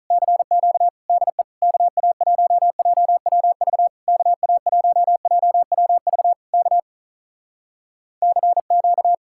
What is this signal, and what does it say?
Morse code, heard as a keyed tone, about 48 wpm, 700 Hz. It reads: CQ DE KA1JWV KA1JWV K CQ